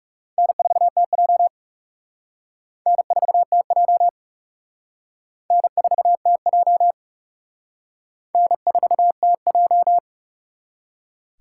Morse code, heard as a keyed tone, about 36 wpm, 690 Hz. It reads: N4TJ N4TJ N4TJ N4TJ
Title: Speed Racing - USA Call Signs 30wpm